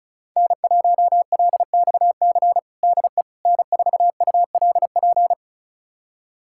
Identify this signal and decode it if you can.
Morse code, heard as a keyed tone, about 34 wpm, 700 Hz. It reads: N1LXC DE N4ULP
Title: USA Callsign DE Callsign - 5 Characters - 35wpm